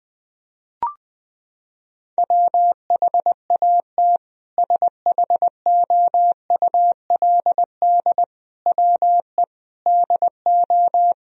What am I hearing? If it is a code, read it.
Morse code, heard as a keyed tone, about 20 wpm, 700 Hz. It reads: WHAT SHOULD WE DO
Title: Sentences from Top 1000 Words - Review Part 4 - 20wpm